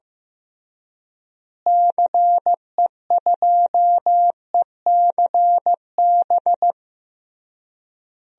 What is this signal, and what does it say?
Morse code, heard as a keyed tone, about 15 wpm, 700 Hz. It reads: CE2ECB
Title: DXCC-100 Call Signs - 6 Characters - Repeated 3x - 15wpm